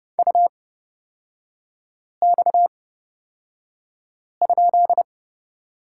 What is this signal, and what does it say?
Morse code, heard as a keyed tone, about 31 wpm, spaced as for 5 wpm, 700 Hz. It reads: UX?
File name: Lesson 38e - Letter q - Three Characters 30wpm